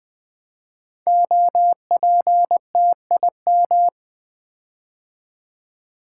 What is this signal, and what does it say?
Morse code, heard as a keyed tone, about 20 wpm, 700 Hz. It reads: OPTIM
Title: Build A Word - 5 Letters At A Time - 10-Letter Words - 20wpm